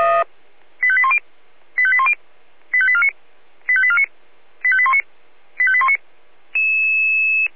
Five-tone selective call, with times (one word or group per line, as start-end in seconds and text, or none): 0.8-1.2 85810
1.8-2.1 85810
2.7-3.1 85849
3.7-4.0 85849
4.6-5.0 85819
5.6-6.0 85819